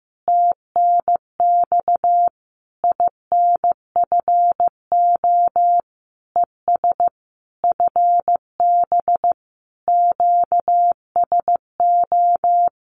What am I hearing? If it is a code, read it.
Morse code, heard as a keyed tone, about 15 wpm, 700 Hz. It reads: TNX INFO ES FB QSO
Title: Common QSO Phrases Encore 15wpm